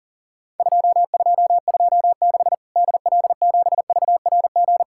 Morse code, heard as a keyed tone, about 40 wpm, 700 Hz. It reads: 2226 DL7VRG
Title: Worked All Europe DX Contest - QTC Exchange - 40wpm